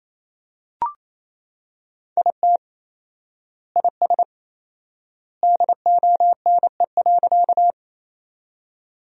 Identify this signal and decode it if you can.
Morse code, heard as a keyed tone, about 28 wpm, 700 Hz. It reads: IT IS DONE.
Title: 4x Word-Spacing - Sentences from Top 500 Words - 28wpm